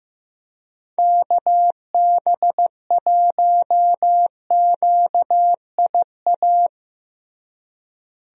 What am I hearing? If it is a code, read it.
Morse code, heard as a keyed tone, about 16 wpm, 700 Hz. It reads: KB1QIA
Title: USA Call Signs - 6 Characters - Repeated 3x - 15wpm